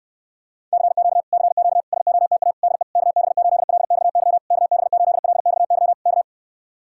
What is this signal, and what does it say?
Morse code, heard as a keyed tone, about 68 wpm, 700 Hz. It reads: CQ CQ SOTA DE KB8LXY KB8LXY K